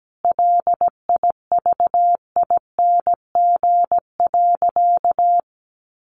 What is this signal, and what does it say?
Morse code, heard as a keyed tone, about 17 wpm, 700 Hz. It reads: LIVING.
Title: Sentences from Top 1500 Words 17wpm